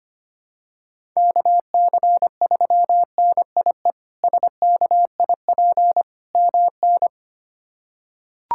Morse code, heard as a keyed tone, about 25 wpm, 700 Hz. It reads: KC3NIE SKIP MN
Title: K1USN SST QSO Practice - 25wpm